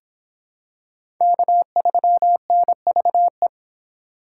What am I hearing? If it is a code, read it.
Morse code, heard as a keyed tone, about 25 wpm, 700 Hz. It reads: K3NVE